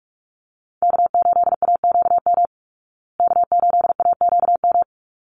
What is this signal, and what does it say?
Morse code, heard as a keyed tone, about 44 wpm, 700 Hz. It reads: K8AQM K8AQM
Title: Call Signs Repeated 3x - 45wpm